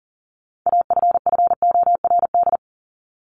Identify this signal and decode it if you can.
Morse code, heard as a keyed tone, about 40 wpm, 700 Hz. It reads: AFFORD
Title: Top 1001-1250 Words 40wpm